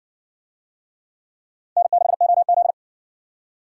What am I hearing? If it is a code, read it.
Morse code, heard as a keyed tone, about 54 wpm, 700 Hz. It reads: N6OZ